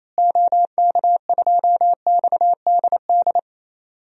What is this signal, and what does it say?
Morse code, heard as a keyed tone, about 28 wpm, 700 Hz. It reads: OK2XDD
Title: DXCC-100 Call Signs - 6 Characters - Repeated 2x - 28wpm